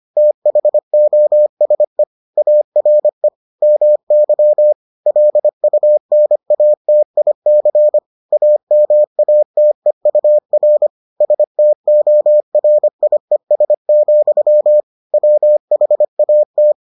Morse code, heard as a keyed tone, about 25 wpm, 600 Hz. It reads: THOSE ARE MY LUNATIC AMATEUR STORIES, WHAT